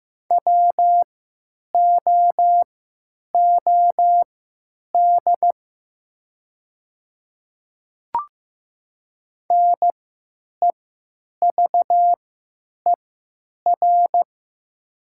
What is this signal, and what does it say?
Morse code, heard as a keyed tone, about 15 wpm, spaced as for 9 wpm, 700 Hz. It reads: WOOD NEVER
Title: 3x Character Spacing - Top 500 Words - 15wpm